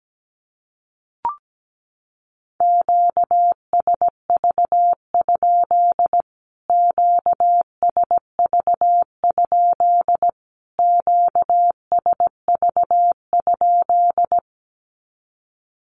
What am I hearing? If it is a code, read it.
Morse code, heard as a keyed tone, about 17 wpm, 700 Hz. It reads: QSV? QSV? QSV?
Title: Q-Codes Repeated 3x 17wpm